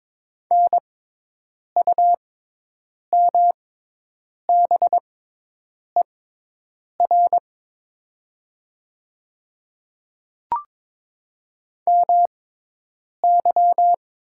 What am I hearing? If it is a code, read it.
Morse code, heard as a keyed tone, about 22 wpm, spaced as for 8 wpm, 700 Hz. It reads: NUMBER MY
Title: 6x Character Spacing - Top 300 Words - 22wpm